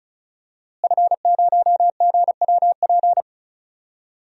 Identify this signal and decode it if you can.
Morse code, heard as a keyed tone, about 34 wpm, 700 Hz. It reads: F0GWP
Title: DXCC-100 Call Signs - 5 Characters - Repeated 2x - 35wpm